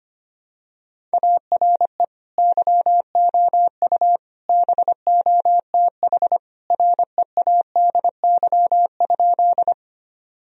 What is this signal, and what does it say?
Morse code, heard as a keyed tone, about 25 wpm, 700 Hz. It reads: ARE YOU BOTH READY?
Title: Sentences from Top 1000 Words - Review Part 1 - 25wpm